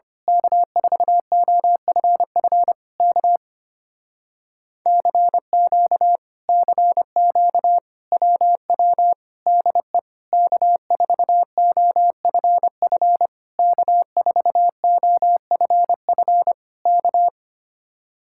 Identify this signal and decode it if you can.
Morse code, heard as a keyed tone, about 24 wpm, 700 Hz. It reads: K4OFF K CQ CQ WW DE K4OFF K4OFF K
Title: Speed-Racing Common QSO Phrases - 25wpm